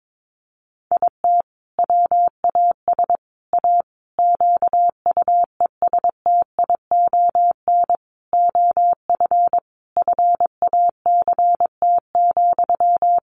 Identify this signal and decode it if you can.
Morse code, heard as a keyed tone, about 22 wpm, 700 Hz. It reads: IT WAS A QUESTION OF FACT,
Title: Sentences from Top 1400 Words 22wpm